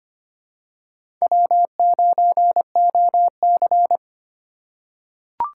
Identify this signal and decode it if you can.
Morse code, heard as a keyed tone, about 25 wpm, 700 Hz. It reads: W9OC
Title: USA Call Signs - 4 Characters - 2x Character Spacing - 25wpm